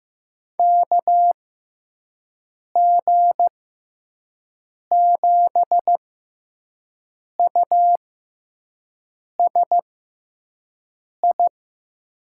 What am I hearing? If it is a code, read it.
Morse code, heard as a keyed tone, about 15 wpm, spaced as for 5 wpm, 700 Hz. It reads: KG7USI
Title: USA Call Signs - 6x Character Spacing - 15wpm